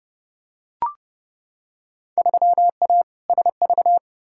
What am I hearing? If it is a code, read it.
Morse code, heard as a keyed tone, about 30 wpm, 700 Hz. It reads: 3A SV
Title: Field Day Exchanges - 30wpm